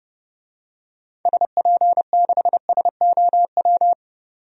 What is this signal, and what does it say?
Morse code, heard as a keyed tone, about 30 wpm, 700 Hz. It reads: SP6SOW
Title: DXCC-100 Call Signs - 6 Characters - 30wpm